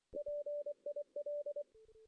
Morse code, heard as a keyed tone, about 23 wpm, 580 Hz. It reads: PIL
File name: 2016 JAN CW NAQP / 2016 Jan naqp k5ot - 0-2